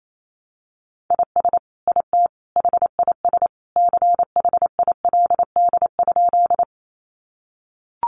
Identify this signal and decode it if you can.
Morse code, heard as a keyed tone, about 28 wpm, 700 Hz. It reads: IS IT HIS CHILD?
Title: Sentences from Top 800 Words 28wpm